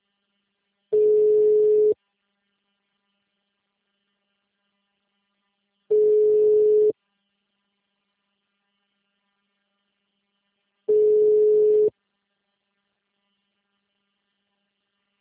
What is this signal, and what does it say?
Ringback tone (Europe, 425 Hz).